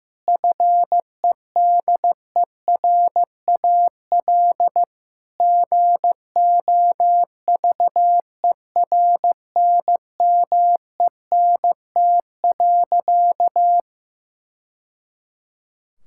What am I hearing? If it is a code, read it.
Morse code, heard as a keyed tone, about 16 wpm, 700 Hz. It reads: FEDERAL GOVERNMENT.
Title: Sentences from Top 1000 Words - Review Part 5 - 15wpm